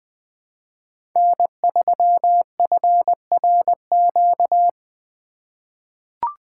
Morse code, heard as a keyed tone, about 20 wpm, 700 Hz. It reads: N3FRQ
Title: USA Call Signs - 5 Characters - 2x Character Spacing - 20wpm